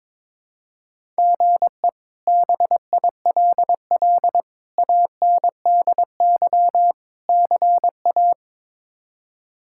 Morse code, heard as a keyed tone, about 22 wpm, 700 Hz. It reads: GE BILL ANDY CA
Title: K1USN SST QSO Practice - 22wpm